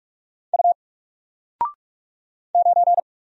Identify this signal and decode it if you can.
Morse code, heard as a keyed tone, about 44 wpm, 700 Hz. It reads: U 9
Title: Single Character (Letter:Number:Punctuation:Prosign) 45wpm